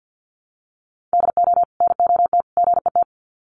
Morse code, heard as a keyed tone, about 50 wpm, 700 Hz. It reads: DO NOT GET
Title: Sets of 3 Words 50wpm